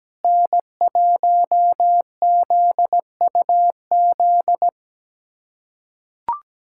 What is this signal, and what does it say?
Morse code, heard as a keyed tone, about 17 wpm, 700 Hz. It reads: N1ZUZ
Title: USA Callsign DE Callsign - 5 Characters - 17wpm